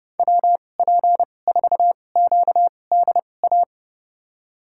Morse code, heard as a keyed tone, about 30 wpm, spaced as for 22 wpm, 700 Hz. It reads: WP4QDA
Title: USA Call Signs - 6 Characters - 2x Character Spacing - 30wpm